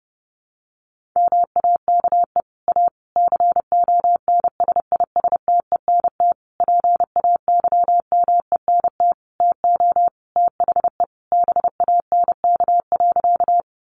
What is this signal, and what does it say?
Morse code, heard as a keyed tone, about 30 wpm, 700 Hz. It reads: MAKE A CONSISTENT PAYMENT TO THE BANK.
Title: Sentences from Top 1500 Words 30wpm